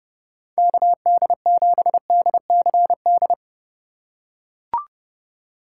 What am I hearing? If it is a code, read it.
Morse code, heard as a keyed tone, about 30 wpm, 700 Hz. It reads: KD7DCD
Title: USA Call Signs - 6 Characters - 4x Character Spacing - 30wpm